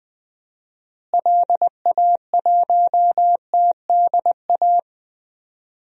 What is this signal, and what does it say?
Morse code, heard as a keyed tone, about 20 wpm, 700 Hz. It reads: LA1TDA
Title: DXCC-100 Call Signs - 6 Characters - Repeated 2x - 20wpm